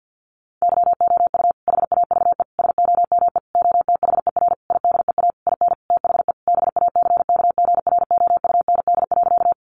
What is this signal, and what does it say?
Morse code, heard as a keyed tone, about 50 wpm, 700 Hz. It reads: YOU HAVE SOME OTHER IDEA IN THE BACKGROUND.
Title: Sentences from Top 1300 Words 50wpm